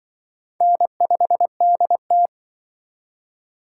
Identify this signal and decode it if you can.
Morse code, heard as a keyed tone, about 25 wpm, 700 Hz. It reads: N5DT
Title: Speed Racing - USA 1x2 Call Signs 20wpm